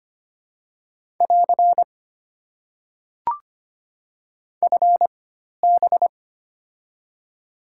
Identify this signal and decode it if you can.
Morse code, heard as a keyed tone, about 25 wpm, 690 Hz. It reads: <AR> F B
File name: Top 90 QSO Elements - 4x Character Spacing - 25wpm